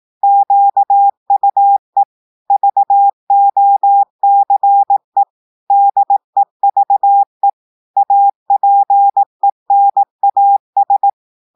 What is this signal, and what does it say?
Morse code, heard as a keyed tone, about 18 wpm, 800 Hz. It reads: QUE VOCE DEVE APENAS
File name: portugais texte en clair 2 vitesse 18 mots minutes